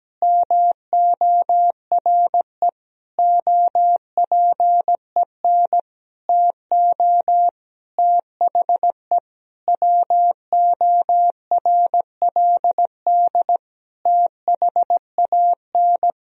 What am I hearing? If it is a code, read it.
Morse code, heard as a keyed tone, about 17 wpm, 700 Hz. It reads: MORE OPEN TO THE WORLD THAN
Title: Sentences from Top 1000 Words - Review Part 3 - 17wpm